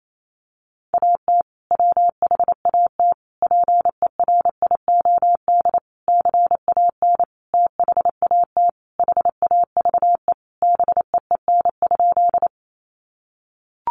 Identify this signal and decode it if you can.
Morse code, heard as a keyed tone, about 28 wpm, 700 Hz. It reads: AT WHAT PERIOD CAN THAT HAVE BEEN?
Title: Sentences from Top 800 Words 28wpm